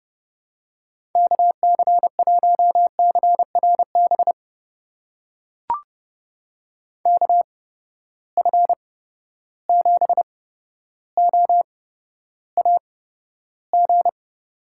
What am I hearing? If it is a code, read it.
Morse code, heard as a keyed tone, about 31 wpm, spaced as for 13 wpm, 700 Hz. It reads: KC1CRB KF7OAG
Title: USA Call Signs - 6 Characters - 8x Character Spacing - 30wpm